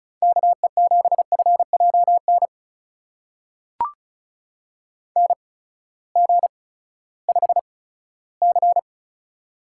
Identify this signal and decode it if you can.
Morse code, heard as a keyed tone, about 36 wpm, spaced as for 18 wpm, 700 Hz. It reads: KE7FJN NG5C
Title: USA Call Signs - 8x Character Spacing - 35wpm